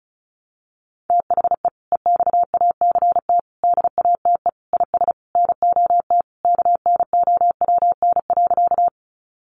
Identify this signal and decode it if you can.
Morse code, heard as a keyed tone, about 34 wpm, 700 Hz. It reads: THE EXACT DATE IS NOT KNOWN.
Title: Sentences from Top 1300 Words 35wpm